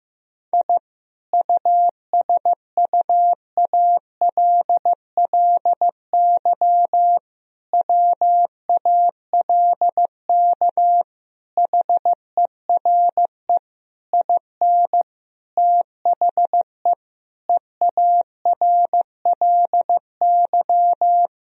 Morse code, heard as a keyed tone, about 15 wpm, 700 Hz. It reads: I USUALLY WALK HERE IN THE EARLY
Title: Sentences from Top 1000 Words - Review Part 3 - 15wpm